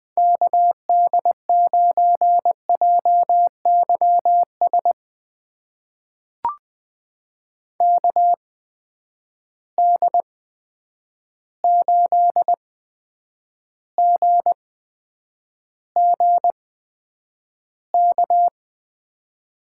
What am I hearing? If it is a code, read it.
Morse code, heard as a keyed tone, about 21 wpm, spaced as for 9 wpm, 700 Hz. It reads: KD9JYS KD8GGK
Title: USA Call Signs - 6 Characters - 8x Character Spacing - 20wpm